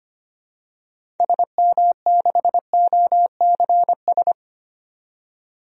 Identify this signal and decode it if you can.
Morse code, heard as a keyed tone, about 25 wpm, 700 Hz. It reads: SM6OCS
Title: DXCC-100 Call Signs - 6 Characters - 25wpm